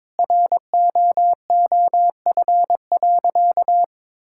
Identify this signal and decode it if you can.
Morse code, heard as a keyed tone, about 22 wpm, 700 Hz. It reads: ROOF.
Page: morse-code-ninja-practice-22wpm